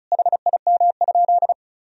Morse code, heard as a keyed tone, about 36 wpm, 700 Hz. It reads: HIM?
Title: Sentences from Top 100 Words and Abbreviations 35wpm